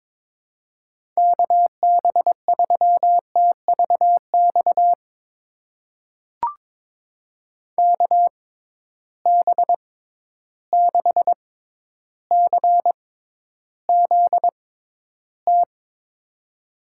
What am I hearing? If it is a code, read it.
Morse code, heard as a keyed tone, about 22 wpm, spaced as for 11 wpm, 700 Hz. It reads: KB3TVX KB6CZT
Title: USA Call Signs - 6 Characters - 6x Character Spacing - 22wpm